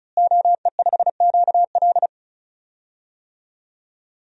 Morse code, heard as a keyed tone, about 36 wpm, 700 Hz. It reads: OE5QL
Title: DXCC-100 Call Signs - 5 Characters - Repeated 2x - 35wpm